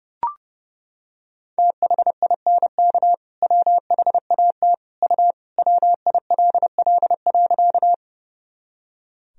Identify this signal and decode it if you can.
Morse code, heard as a keyed tone, about 32 wpm, 700 Hz. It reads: THINK WHAT U WILL.
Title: Sentences from Top 200 Words and Abbreviations 30wpm